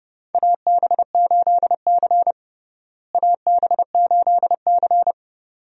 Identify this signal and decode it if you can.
Morse code, heard as a keyed tone, about 30 wpm, 700 Hz. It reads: AB8C AB8C
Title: USA Call Signs - 4 Characters - Repeated 2x - 30wpm